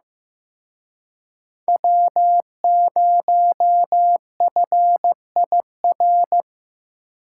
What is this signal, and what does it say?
Morse code, heard as a keyed tone, about 18 wpm, 700 Hz. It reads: W0FIR
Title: Speed Racing - USA 1x3 Call Signs 15wpm